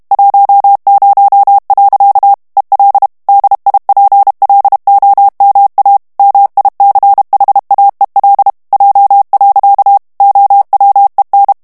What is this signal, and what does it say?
Morse code, heard as a keyed tone, about 32 wpm, 800 Hz. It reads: 10. EL DIPLOMA MICHAEL J. OWEN
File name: morse mp3 / ESPAGNOL TEXTE EN CLAIR 1 VITESSE 30 MOTS MINUTES